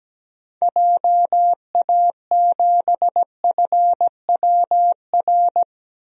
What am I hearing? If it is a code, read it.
Morse code, heard as a keyed tone, about 17 wpm, 700 Hz. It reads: JA7FWR
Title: DXCC-100 Call Signs - 6 Characters - 17wpm